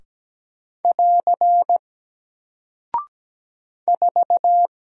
Morse code, heard as a keyed tone, about 17 wpm, 700 Hz. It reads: <AR> 4
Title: Single Character (Letter:Number:Punctuation:Prosign) 17wpm